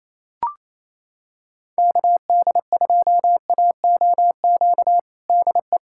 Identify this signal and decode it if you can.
Morse code, heard as a keyed tone, about 28 wpm, 700 Hz. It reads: KD2AOQ DE
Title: USA Callsign DE Callsign - 6 Characters - 28wpm